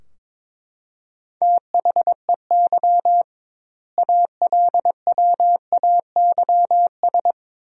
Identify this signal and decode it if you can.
Morse code, heard as a keyed tone, about 22 wpm, 700 Hz. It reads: THEY ALWAYS
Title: 2x Word-Spacing - Sentences from Top 300 Words - 22wpm